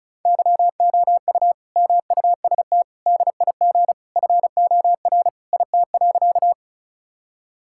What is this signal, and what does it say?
Morse code, heard as a keyed tone, about 34 wpm, 700 Hz. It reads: YOU MUST DIG FOR IT.
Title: Sentences from Top 1600 Words 35wpm